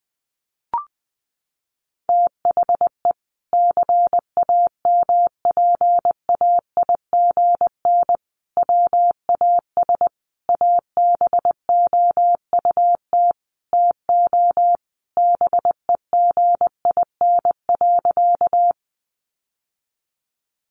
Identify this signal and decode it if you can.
Morse code, heard as a keyed tone, about 20 wpm, 700 Hz. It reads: THE CAMPAIGN WAS ABOUT TO BEGIN.
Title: Sentences from Top 1200 Words 20wpm